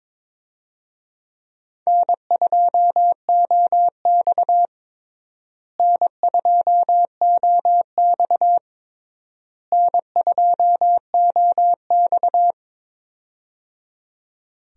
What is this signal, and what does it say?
Morse code, heard as a keyed tone, about 22 wpm, 700 Hz. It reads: N2OX N2OX N2OX
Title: USA Call Signs - 4 Characters - Repeated 3x - 22wpm